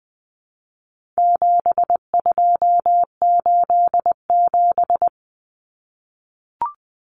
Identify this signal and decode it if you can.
Morse code, heard as a keyed tone, about 20 wpm, 700 Hz. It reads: 7287
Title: Common Cut Numbers - 4-Digit 20wpm